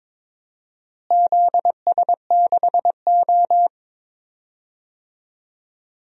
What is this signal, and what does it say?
Morse code, heard as a keyed tone, about 22 wpm, 700 Hz. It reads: ZS6O
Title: DXCC-100 Call Signs - 4 Characters - Repeated 3x - 22wpm